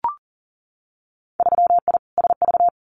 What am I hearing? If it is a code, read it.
Morse code, heard as a keyed tone, about 39 wpm, 700 Hz. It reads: 3I SV
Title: Winter Field Day Exchanges Encore 40wpm